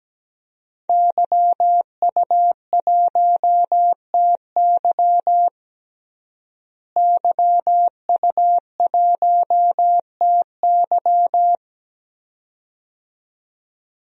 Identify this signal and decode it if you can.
Morse code, heard as a keyed tone, about 17 wpm, 700 Hz. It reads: YU1TY YU1TY
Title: DXCC-100 Call Signs - 5 Characters - Repeated 2x - 17wpm